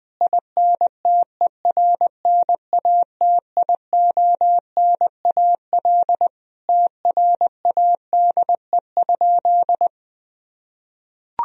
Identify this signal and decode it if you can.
Morse code, heard as a keyed tone, about 21 wpm, 700 Hz. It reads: INTERNATIONAL TRADE?
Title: Sentences from Top 1000 Words - Review Part 4 - 20wpm